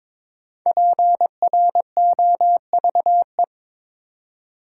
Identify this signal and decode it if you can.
Morse code, heard as a keyed tone, about 22 wpm, 700 Hz. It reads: PROVE
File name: 5 Letter Words - First 3 Letters Given - 22wpm